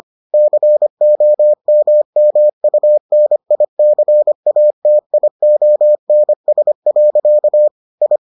Morse code, heard as a keyed tone, about 25 wpm, 600 Hz. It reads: COMMUNICATIONS. I